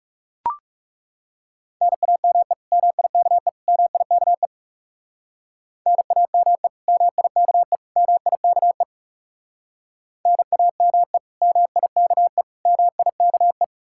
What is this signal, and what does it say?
Morse code, heard as a keyed tone, about 38 wpm, 700 Hz. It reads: NAME MIKE MIKE NAME MIKE MIKE NAME MIKE MIKE